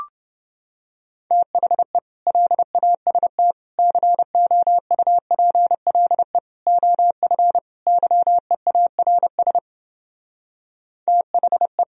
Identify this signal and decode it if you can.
Morse code, heard as a keyed tone, about 29 wpm, 700 Hz. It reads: THE LAST COUPLE OF YEARS THE